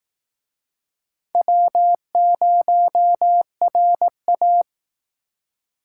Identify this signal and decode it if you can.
Morse code, heard as a keyed tone, about 19 wpm, 700 Hz. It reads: W0RA